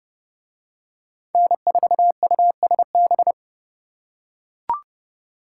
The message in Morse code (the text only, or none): N4USB